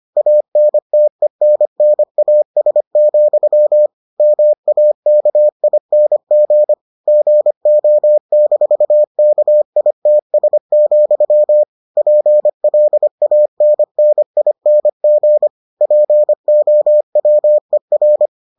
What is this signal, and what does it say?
Morse code, heard as a keyed tone, about 25 wpm, 600 Hz. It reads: ANTENNAS, MAKING GO-KITS, PLANNING POWER